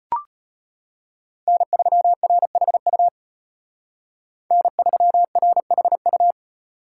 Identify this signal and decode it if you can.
Morse code, heard as a keyed tone, about 32 wpm, 700 Hz. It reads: N3RHU N3RHU